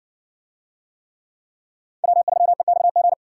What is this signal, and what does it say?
Morse code, heard as a keyed tone, about 55 wpm, 700 Hz. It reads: W3EXG